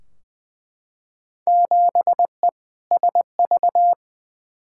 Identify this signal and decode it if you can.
Morse code, heard as a keyed tone, about 20 wpm, 700 Hz. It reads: 7E SV
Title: Field Day Exchanges Encore - 20wpm